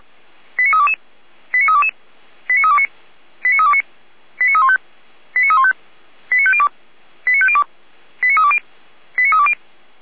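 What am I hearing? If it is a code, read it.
Five-tone selective call (ZVEI), heard as quick runs of five tones, tones of about 70 ms. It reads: 89322 89322 89329 89329 89315 89315 89502 89502 89320 89320
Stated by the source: ZVEI five-tone call received by SDR.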